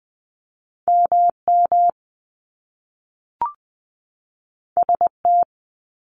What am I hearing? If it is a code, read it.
Morse code, heard as a keyed tone, about 20 wpm, 700 Hz. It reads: M M S T
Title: Top Two Letter Combinations in Words 20wpm